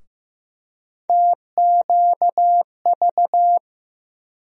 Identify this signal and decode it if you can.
Morse code, heard as a keyed tone, about 15 wpm, 700 Hz. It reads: TQV
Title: USA Call Sign Postfixes - 3 Letters 15wpm